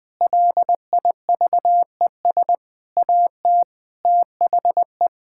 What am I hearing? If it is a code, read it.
Morse code, heard as a keyed tone, about 20 wpm, 700 Hz. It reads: LIVES AT THE
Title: Sentences with 5-Word Phrases - 20wpm